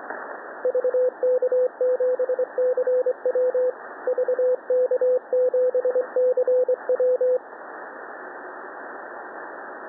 Morse code, heard as a keyed tone, about 24 wpm, 500 Hz, with radio noise. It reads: VK7CW VK7CW
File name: TN5E wkg VK7CW 20CW